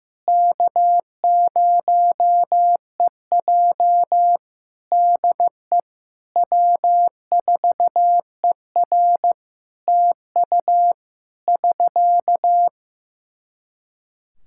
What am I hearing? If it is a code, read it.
Morse code, heard as a keyed tone, about 15 wpm, 700 Hz. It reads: K0EJ DE W4ER TU <SK>
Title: Practice QSO - Exchange 4 - Ending - 15wpm